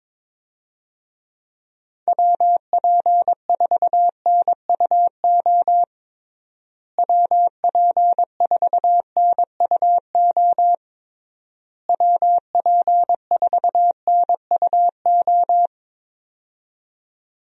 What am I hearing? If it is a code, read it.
Morse code, heard as a keyed tone, about 22 wpm, 700 Hz. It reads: WP4NUO WP4NUO WP4NUO
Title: USA Call Signs - 6 Characters - Repeated 3x - 22wpm